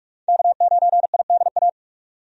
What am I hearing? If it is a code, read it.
Morse code, heard as a keyed tone, about 47 wpm, 700 Hz. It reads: K9IDA
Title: USA Call Signs - 5 Characters - 45wpm